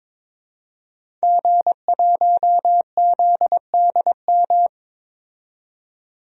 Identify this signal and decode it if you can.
Morse code, heard as a keyed tone, about 22 wpm, 700 Hz. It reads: G1ZDM
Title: DXCC-100 Call Signs - 5 Characters - 22wpm